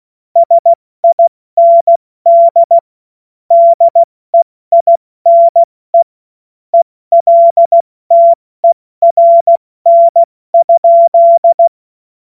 Morse code, heard as a keyed tone, about 16 wpm, 680 Hz. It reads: SIND DEINE ELTERN?
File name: Morsekurs / Witze / Witz 12. Oktober 2021, 70 F80